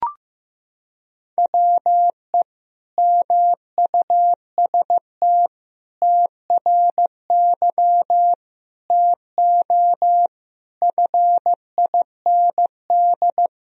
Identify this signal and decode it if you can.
Morse code, heard as a keyed tone, about 15 wpm, 700 Hz. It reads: WE MUST TRY TO FIND
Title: Sentences from Top 400 Words and Abbreviations 15wpm